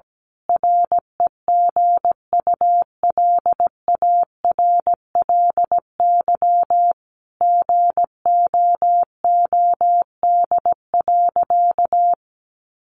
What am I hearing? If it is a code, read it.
Morse code, heard as a keyed tone, about 17 wpm, 700 Hz. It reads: REGULARLY GOOD.
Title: Sentences from Top 1400 Words 17wpm